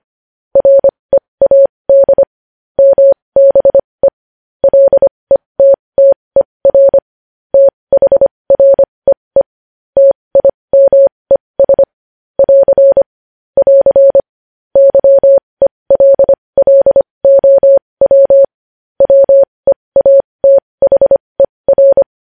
Morse code, heard as a keyed tone, about 25 wpm, spaced as for 20 wpm, 550 Hz. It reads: READ MBE LETTER THREE TIMES <AR> <AR> YELLOW WEATHER
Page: News headlines in Morse code at 20 WPM